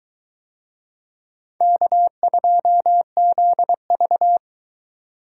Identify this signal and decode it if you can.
Morse code, heard as a keyed tone, about 22 wpm, 700 Hz. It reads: K2ZV